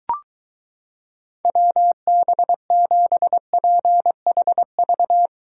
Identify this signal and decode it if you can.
Morse code, heard as a keyed tone, about 23 wpm, 700 Hz. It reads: WB7PHV